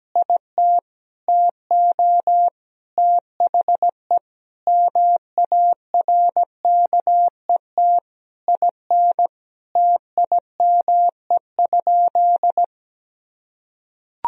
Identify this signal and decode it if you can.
Morse code, heard as a keyed tone, about 17 wpm, 700 Hz. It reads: IT TO THE MARKET IN TIME?
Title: Sentences from Top 1000 Words - Review Part 3 - 17wpm